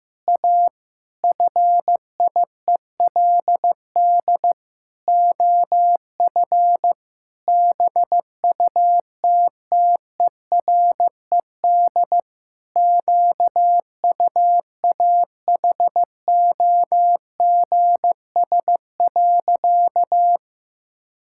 Morse code, heard as a keyed tone, about 15 wpm, 700 Hz. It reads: A FIELD OF BUTTERED QUAHOGS.
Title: Pangram Challenge v2 - 15wpm